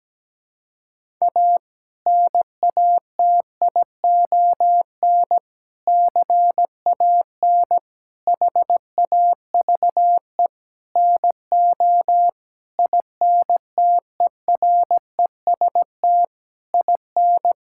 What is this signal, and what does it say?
Morse code, heard as a keyed tone, about 17 wpm, 700 Hz. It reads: A NATION CAN HAVE NO INTEREST IN
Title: Sentences from Top 1000 Words - Review Part 6 - 17wpm